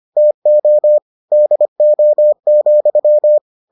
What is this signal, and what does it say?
Morse code, heard as a keyed tone, about 25 wpm, 600 Hz. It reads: TO DO,